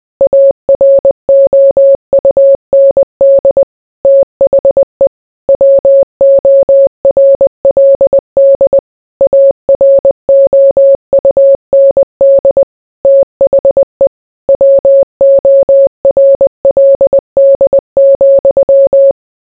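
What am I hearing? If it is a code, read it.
Morse code, heard as a keyed tone, about 20 wpm, 550 Hz. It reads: AROUND THE WORLD AROUND THE WORLD,
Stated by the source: Morse code mixed with noise and music as background sound for sleep.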